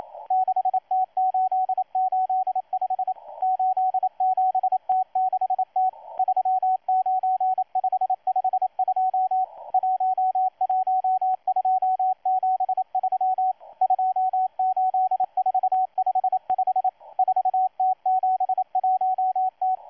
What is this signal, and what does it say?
Morse code, heard as a keyed tone, about 27 wpm, 750 Hz, with radio noise. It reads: 6T885 87T6T 39552 11273 28455 4T71T
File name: M12-2016-07-07-1310utc-13926khz-error-restart